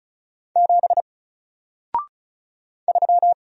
Morse code, heard as a keyed tone, about 36 wpm, 700 Hz. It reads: 7 3
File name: Single-Letter-Number-35wpm